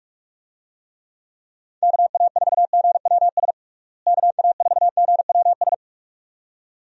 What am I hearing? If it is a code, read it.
Morse code, heard as a keyed tone, about 46 wpm, 700 Hz. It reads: KA4GWS KA4GWS
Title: USA Call Signs - 6 Characters - Repeated 2x - 45wpm